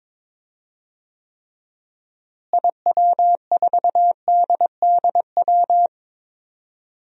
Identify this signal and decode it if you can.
Morse code, heard as a keyed tone, about 21 wpm, 700 Hz. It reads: IW4DDW
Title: DXCC-100 Call Signs - 6 Characters - Repeated 2x - 22wpm